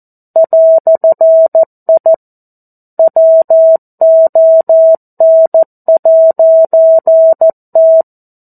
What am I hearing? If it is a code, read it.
Morse code, heard as a keyed tone, about 14 wpm, 650 Hz. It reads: "I WON'T
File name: 02-Chapter-02 14 WPM